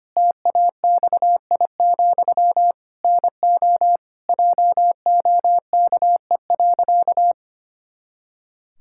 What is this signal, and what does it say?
Morse code, heard as a keyed tone, about 25 wpm, 700 Hz. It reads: TAXI, NO JOKE.
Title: Pangram Challenge v2 - 25wpm